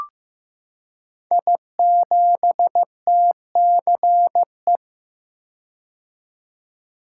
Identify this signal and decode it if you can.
Morse code, heard as a keyed tone, about 15 wpm, 700 Hz. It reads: I7TCE
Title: Code Groups - 5 Letter:Numbers - 15wpm